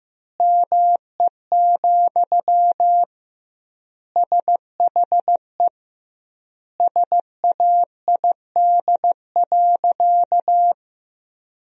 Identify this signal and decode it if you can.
Morse code, heard as a keyed tone, about 16 wpm, 700 Hz. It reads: ME, SHE SAID.
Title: 2x Word-Spacing - Sentences from Top 500 Words - 15wpm